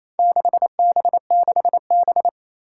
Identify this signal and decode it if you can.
Morse code, heard as a keyed tone, about 28 wpm, 700 Hz. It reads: 6B6B
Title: Four Challenging Letter-Number Combinations 28wpm